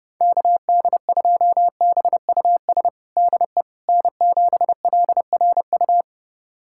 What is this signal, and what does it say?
Morse code, heard as a keyed tone, about 32 wpm, 700 Hz. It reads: KD2BUS DE N7LRU
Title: USA Callsign DE Callsign 30wpm